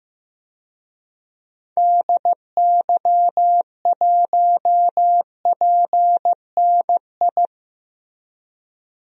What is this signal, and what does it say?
Morse code, heard as a keyed tone, about 15 wpm, 700 Hz. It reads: DY1PNI
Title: DXCC-100 Call Signs - 6 Characters - Repeated 2x - 15wpm